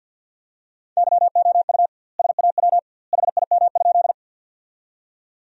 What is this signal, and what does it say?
Morse code, heard as a keyed tone, about 49 wpm, 700 Hz. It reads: YOU SAW HIM?
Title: 2x Word-Spacing - Sentences from Top 300 Words - 50wpm